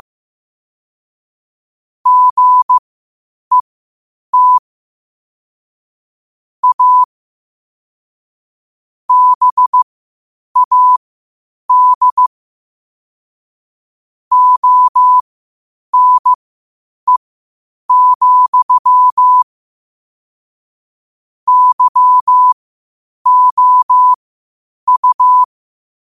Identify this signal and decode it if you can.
Morse code, heard as a keyed tone, about 15 wpm, spaced as for 8 wpm, 1000 Hz. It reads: GET A BAD ONE, YOU